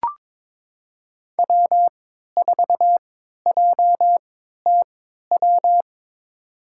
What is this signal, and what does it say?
Morse code, heard as a keyed tone, about 22 wpm, spaced as for 12 wpm, 700 Hz. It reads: W4JTW
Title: USA Call Signs - 5 Characters - 3x Character Spacing - 22wpm